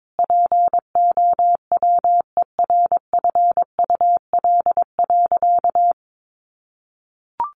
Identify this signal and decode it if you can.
Morse code, heard as a keyed tone, about 22 wpm, 700 Hz. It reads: POWERFUL.
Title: Sentences from Top 1300 Words 22wpm